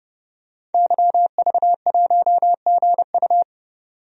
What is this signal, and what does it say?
Morse code, heard as a keyed tone, about 30 wpm, 700 Hz. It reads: YV1GU
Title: DXCC-100 Call Signs - 5 Characters - 30wpm